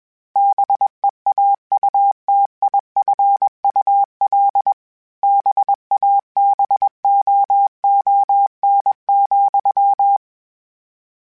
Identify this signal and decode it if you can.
Morse code, heard as a keyed tone, about 21 wpm, 800 Hz. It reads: BEAUTIFUL BABOON,